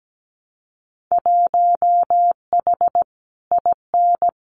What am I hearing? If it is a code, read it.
Morse code, heard as a keyed tone, about 17 wpm, 700 Hz. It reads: 1H IN
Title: Winter Field Day Exchanges 17wpm